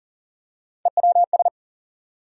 Morse code, heard as a keyed tone, about 40 wpm, 700 Hz. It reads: EWS
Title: USA Call Sign Postfixes - 3 Letters 40wpm